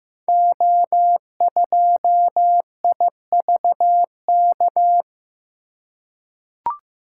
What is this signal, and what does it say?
Morse code, heard as a keyed tone, about 15 wpm, 700 Hz. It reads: O2IVK
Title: Code Groups - 5 Letter:Numbers - 15wpm